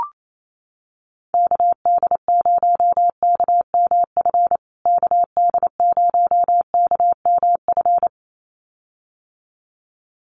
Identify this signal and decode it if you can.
Morse code, heard as a keyed tone, about 28 wpm, 700 Hz. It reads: KD0KMF KD0KMF
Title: USA Call Signs Repeated 2x 28wpm